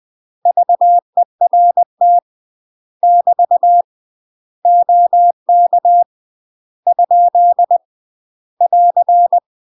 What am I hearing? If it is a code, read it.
Morse code, heard as a keyed tone, about 20 wpm, 700 Hz. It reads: VERT <BT> OK ? <AR>
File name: UK MASTER QSOs TRAINING / 20WPM / UK MASTER QSOs 0045